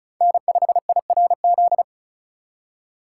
Morse code, heard as a keyed tone, about 36 wpm, 700 Hz. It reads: N5IRZ
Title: USA Call Signs - 5 Characters - 3x Character Spacing - 35wpm